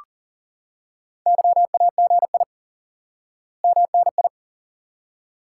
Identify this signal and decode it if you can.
Morse code, heard as a keyed tone, about 37 wpm, 700 Hz. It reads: YAGI MNI
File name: IQE Course 3 - Lesson 89 - YAGI - Two Elements - 40wpm